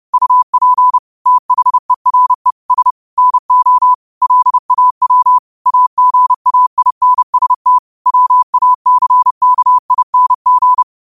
Morse code, heard as a keyed tone, about 31 wpm, 1000 Hz. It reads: UP THERES NO LAW AGAINST WACKING
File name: qotdfiles / 30wpm